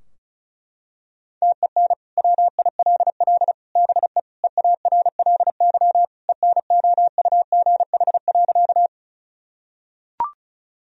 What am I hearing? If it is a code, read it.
Morse code, heard as a keyed tone, about 36 wpm, 700 Hz. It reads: TEN WILL BE EARLY ENOUGH.
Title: Sentences from Top 1000 Words - Review Part 2 - 35wpm